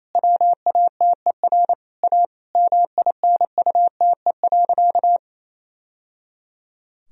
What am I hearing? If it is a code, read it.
Morse code, heard as a keyed tone, about 28 wpm, 700 Hz. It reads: WATER A MINUTE.
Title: Sentences from Top 1000 Words - Review Part 1 - 28wpm